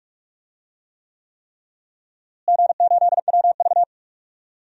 Morse code, heard as a keyed tone, about 41 wpm, 700 Hz. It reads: G8WV
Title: DXCC-100 Call Signs - 4 Characters - Repeated 3x - 45wpm